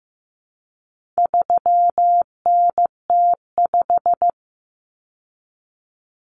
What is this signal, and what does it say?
Morse code, heard as a keyed tone, about 15 wpm, 700 Hz. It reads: 3NT5
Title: Common Cut Numbers - 4-Digit 15wpm